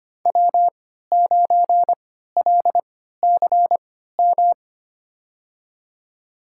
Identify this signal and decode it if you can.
Morse code, heard as a keyed tone, about 25 wpm, spaced as for 14 wpm, 700 Hz. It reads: W9LCM
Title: USA Call Signs - 5 Characters - 3x Character Spacing - 25wpm